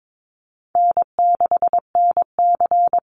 Morse code, heard as a keyed tone, about 22 wpm, 700 Hz. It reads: N6NC
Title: Call Signs Repeated 3x - 22wpm